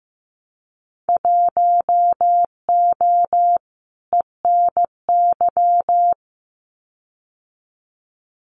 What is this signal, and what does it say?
Morse code, heard as a keyed tone, about 15 wpm, 700 Hz. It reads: 1O ENY
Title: Winter Field Day Exchanges 15wpm